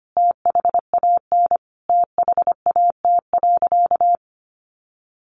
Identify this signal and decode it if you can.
Morse code, heard as a keyed tone, about 25 wpm, 700 Hz. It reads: THAN THAT.
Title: Sentences from Top 400 Words 25wpm